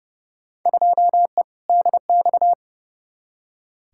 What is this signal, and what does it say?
Morse code, heard as a keyed tone, about 30 wpm, 700 Hz. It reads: 2E DX
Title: Field Day Exchanges - 30wpm